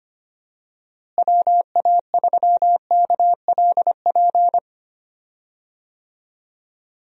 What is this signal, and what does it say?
Morse code, heard as a keyed tone, about 25 wpm, 700 Hz. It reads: WA3KLP
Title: USA Call Signs - 6 Characters - Repeated 2x - 25wpm